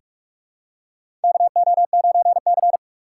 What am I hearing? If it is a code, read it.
Morse code, heard as a keyed tone, about 46 wpm, 700 Hz. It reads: KO9C
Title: USA Call Signs - 4 Characters - 3x Character Spacing - 45wpm